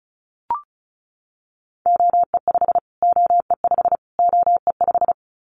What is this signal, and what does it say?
Morse code, heard as a keyed tone, about 34 wpm, 700 Hz. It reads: OE5 OE5 OE5
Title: Callsign Prefixes for Top 25 DXCC - Repeated 3x - 35wpm